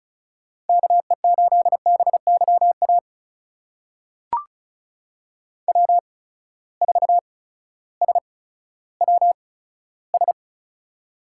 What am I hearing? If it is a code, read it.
Morse code, heard as a keyed tone, about 34 wpm, spaced as for 16 wpm, 700 Hz. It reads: KE8BYA W4SWS
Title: USA Call Signs - 8x Character Spacing - 35wpm